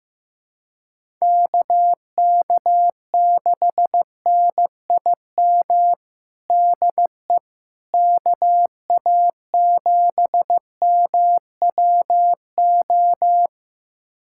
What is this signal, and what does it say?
Morse code, heard as a keyed tone, about 15 wpm, 700 Hz. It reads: KK6NIM DE KA7MWO
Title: USA Callsign DE Callsign - 6 Characters - 15wpm